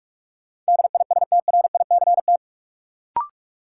Morse code, heard as a keyed tone, about 46 wpm, 700 Hz. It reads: DISTRICT
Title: Sentences from Top 1600 Words - New Word Drill Alt - 45wpm